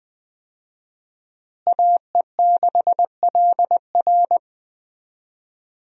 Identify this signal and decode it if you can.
Morse code, heard as a keyed tone, about 20 wpm, 700 Hz. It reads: AE6LR
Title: USA Call Signs - 5 Characters - Repeated 2x - 20wpm